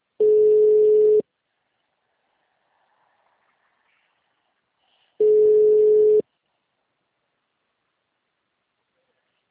Ringback tone (Europe, 425 Hz).